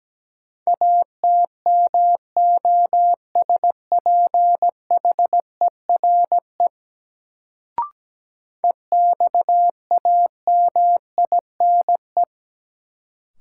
Morse code, heard as a keyed tone, about 17 wpm, 700 Hz. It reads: ATMOSPHERE EXAMINE
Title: Sentences from Top 1600 Words - New Word Drill - 17wpm